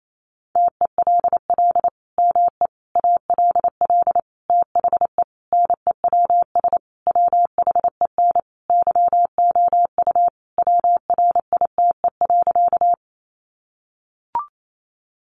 Morse code, heard as a keyed tone, about 28 wpm, 700 Hz. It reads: TELL ME ALL THE NEWS WHEN YOU WRITE.
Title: Sentences from Top 800 Words 28wpm